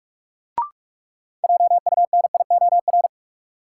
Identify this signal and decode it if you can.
Morse code, heard as a keyed tone, about 42 wpm, 700 Hz. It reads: JUNIOR
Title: Sentences from Top 1500 Words - New Word Drill Alt - 45wpm